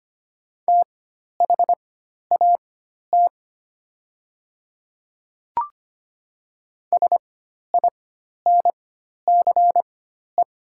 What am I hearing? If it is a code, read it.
Morse code, heard as a keyed tone, about 25 wpm, spaced as for 12 wpm, 700 Hz. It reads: THAT SINCE
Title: 4x Character Spacing - Top 300 Words - 25wpm